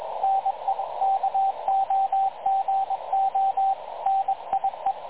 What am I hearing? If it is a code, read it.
Morse code, heard as a keyed tone, about 23 wpm, 780 Hz, with radio noise. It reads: NE KOGO NIE
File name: dlazaawansowanych / nagraniazpasma / websdr recording 3567.0kHz